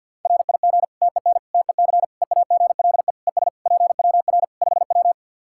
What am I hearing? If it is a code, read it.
Morse code, heard as a keyed tone, about 50 wpm, 700 Hz. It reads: RIG TEN TEC EAGLE ES PWR 5W
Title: Practice QSO - Exchange 2 - Info - 50wpm